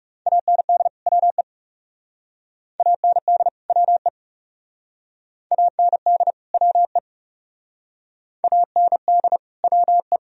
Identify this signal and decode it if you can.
Morse code, heard as a keyed tone, about 37 wpm, 700 Hz. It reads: AND WE AND WE AND WE AND WE
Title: Speed-Racing - Sets of 2 Words 30wpm